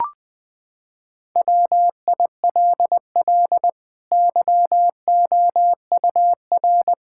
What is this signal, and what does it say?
Morse code, heard as a keyed tone, about 21 wpm, 700 Hz. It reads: WILL YOUR
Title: Sentences from Top 1000 Words - Review Part 5 - 20wpm